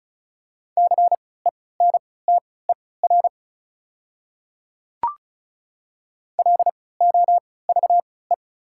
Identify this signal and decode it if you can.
Morse code, heard as a keyed tone, about 36 wpm, spaced as for 20 wpm, 690 Hz. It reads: CENTER LOVE
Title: 3x Character Spacing - Top 500 Words - 35wpm